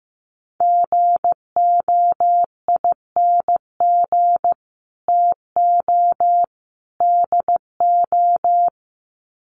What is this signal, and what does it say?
Morse code, heard as a keyed tone, about 15 wpm, 700 Hz. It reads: GOING TO DO
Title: Sets of 3 Words 15wpm